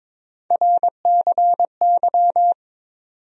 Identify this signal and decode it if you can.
Morse code, heard as a keyed tone, about 21 wpm, 700 Hz. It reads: RCY
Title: USA Call Sign Postfixes - 3 Letters 22wpm